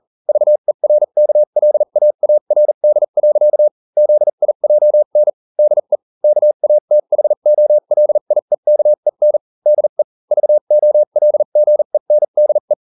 Morse code, heard as a keyed tone, about 40 wpm, spaced as for 35 wpm, 600 Hz. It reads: VERKLAARD. ZIJN DE KATHOLIEKEN DE VOLGENDE